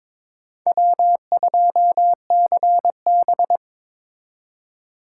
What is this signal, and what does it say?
Morse code, heard as a keyed tone, about 22 wpm, 700 Hz. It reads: W2CB
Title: USA Call Signs - 4 Characters - 4x Character Spacing - 22wpm